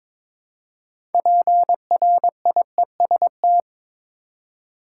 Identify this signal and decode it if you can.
Morse code, heard as a keyed tone, about 21 wpm, 700 Hz. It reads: PRIEST
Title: Six Letter Words - 22wpm